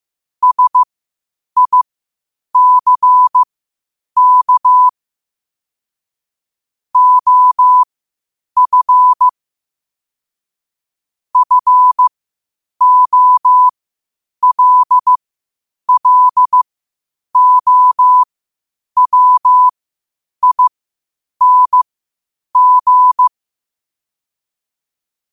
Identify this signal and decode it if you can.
Morse code, heard as a keyed tone, about 15 wpm, spaced as for 8 wpm, 1000 Hz. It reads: SICK OF FOLLOWING
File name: qotdfiles / 8wpm